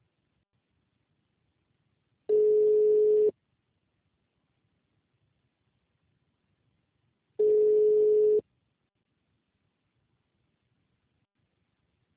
Ringback tone (Europe, 425 Hz).
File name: out- -2010- - - .27286